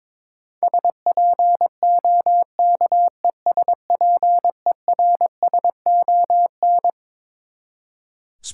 Morse code, heard as a keyed tone, about 22 wpm, 700 Hz. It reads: SPOKESPERSON
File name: Twelve Letter Words - 22wpm